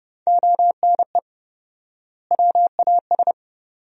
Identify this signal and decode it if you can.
Morse code, heard as a keyed tone, about 30 wpm, 700 Hz. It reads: ONE WAS
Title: 4x Word-Spacing - Sentences from Top 300 Words - 30wpm